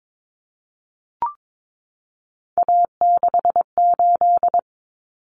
Morse code, heard as a keyed tone, about 22 wpm, 700 Hz. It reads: A68
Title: Common Cut Numbers - 3-Digit 22wpm